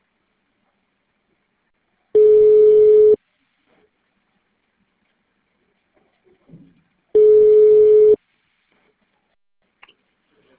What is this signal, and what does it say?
Ringback tone (Europe, 425 Hz).